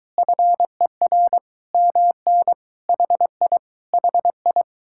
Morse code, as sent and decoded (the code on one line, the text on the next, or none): ..-. .-. / -- -. / .... .. / .... ..
FER MN HI HI